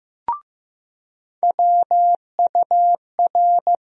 Morse code, heard as a keyed tone, about 15 wpm, 700 Hz. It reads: WUR
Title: USA Call Sign Postfixes - 3 Letters 15wpm